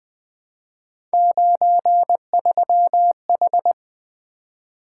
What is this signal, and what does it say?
Morse code, heard as a keyed tone, about 20 wpm, 700 Hz. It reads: 93H
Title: Three Letter-Number Combinations - 20wpm